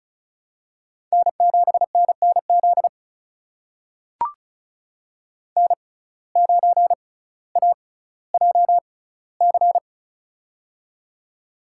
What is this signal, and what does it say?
Morse code, heard as a keyed tone, about 36 wpm, spaced as for 18 wpm, 700 Hz. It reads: N7NNZ N9AJC
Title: USA Call Signs - 5 Characters - 6x Character Spacing - 35wpm